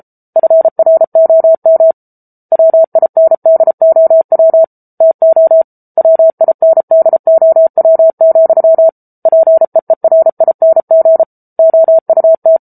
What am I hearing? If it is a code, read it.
Morse code, heard as a keyed tone, about 34 wpm, 650 Hz. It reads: FROM WINDOW TO WINDOW, PEERING OUT